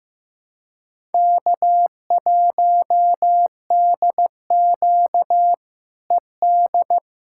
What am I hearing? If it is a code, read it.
Morse code, heard as a keyed tone, about 15 wpm, 700 Hz. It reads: K1DQ ED
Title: K1USN SST QSO Practice - 15wpm